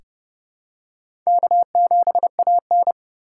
Morse code, heard as a keyed tone, about 29 wpm, 700 Hz. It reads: K7AN
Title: USA Call Signs - 2x Character Spacing - 30wpm